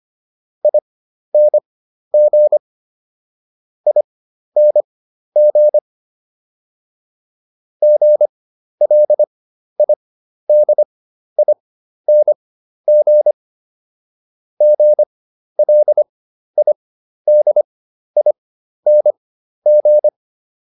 Morse code, heard as a keyed tone, about 25 wpm, spaced as for 12 wpm, 600 Hz. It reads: ING ING GLIDING GLIDING